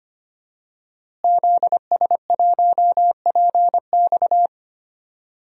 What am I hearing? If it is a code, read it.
Morse code, heard as a keyed tone, about 25 wpm, 700 Hz. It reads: ZS1PX
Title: DXCC-100 Call Signs - 5 Characters - Repeated 2x - 25wpm